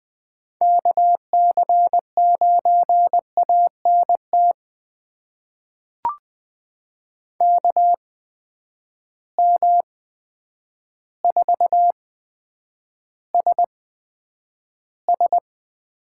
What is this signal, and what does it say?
Morse code, heard as a keyed tone, about 20 wpm, spaced as for 9 wpm, 700 Hz. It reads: KC9ANT KM4SS
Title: USA Call Signs - 6 Characters - 8x Character Spacing - 20wpm